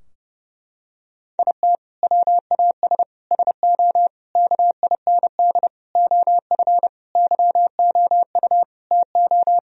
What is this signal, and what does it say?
Morse code, heard as a keyed tone, about 32 wpm, 700 Hz. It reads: IT WAS SO KIND OF YOU TO
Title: Sentences from Top 1000 Words - Review Part 3 - 30wpm